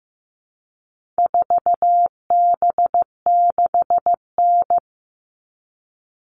Morse code, heard as a keyed tone, about 15 wpm, 700 Hz. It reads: 4B6N
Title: All Cut Numbers - 4-Digit 15wpm